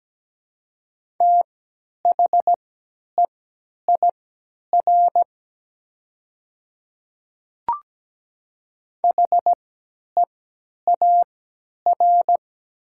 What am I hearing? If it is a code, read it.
Morse code, heard as a keyed tone, about 17 wpm, spaced as for 10 wpm, 700 Hz. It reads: THEIR HEAR
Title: 3x Character Spacing - Top 300 Words - 17wpm